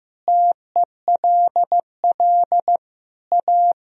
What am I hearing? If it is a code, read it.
Morse code, heard as a keyed tone, about 15 wpm, 700 Hz. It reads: TELL A
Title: Sentences from Top 1000 Words - Review Part 4 - 15wpm